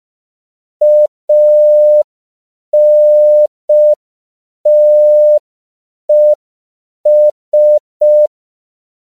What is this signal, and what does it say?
Morse code, heard as a keyed tone, about 5 wpm, 600 Hz. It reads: ANTES
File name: CW / 5PPM